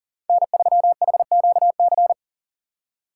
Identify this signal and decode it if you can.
Morse code, heard as a keyed tone, about 49 wpm, 700 Hz. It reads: N3HQC
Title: Speed Racing - USA 1x3 Call Signs 40wpm